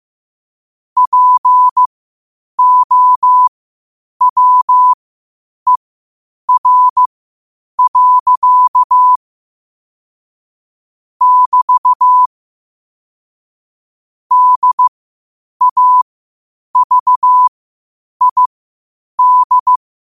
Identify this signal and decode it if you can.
Morse code, heard as a keyed tone, about 15 wpm, spaced as for 8 wpm, 1000 Hz. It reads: POWER. <BT> DAVID
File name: qotdfiles / 8wpm